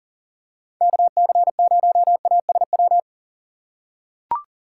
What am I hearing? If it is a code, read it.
Morse code, heard as a keyed tone, about 40 wpm, 700 Hz. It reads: KC0ASW
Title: USA Call Signs - 6 Characters - 2x Character Spacing - 40wpm